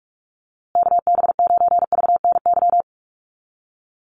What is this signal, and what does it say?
Morse code, heard as a keyed tone, about 44 wpm, 700 Hz. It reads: KB9VNY
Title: USA Call Signs Repeated 3x 45wpm